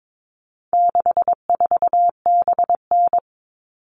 Morse code, heard as a keyed tone, about 22 wpm, 700 Hz. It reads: 64BN
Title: All Cut Numbers - 4-Digit 22wpm